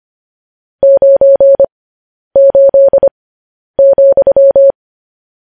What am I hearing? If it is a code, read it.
Morse code, heard as a keyed tone, about 25 wpm, spaced as for 10 wpm, 550 Hz. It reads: 98,
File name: podcast / News headlines-1x-25x10